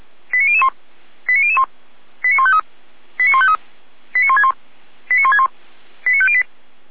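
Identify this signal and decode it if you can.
Five-tone selective call (ZVEI), heard as quick runs of five tones, tones of about 70 ms. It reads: 89001 89001 89153 89153 89161 89161 89508
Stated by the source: ZVEI five-tone call received by SDR, from feuerwehr-orlamuende.org (fire brigade Orlamünde).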